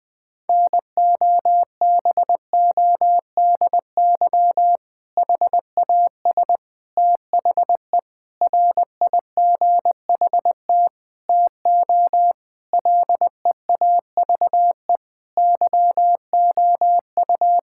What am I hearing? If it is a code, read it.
Morse code, heard as a keyed tone, about 20 wpm, 700 Hz. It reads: NOBODY HAS THE RIGHT TO LEAVE YOU
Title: Sentences from Top 1600 Words 20wpm